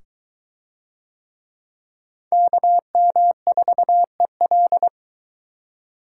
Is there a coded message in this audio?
Yes: Morse code from 2.3 to 4.9 s, heard as a keyed tone, about 23 wpm, 700 Hz. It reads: KM4EL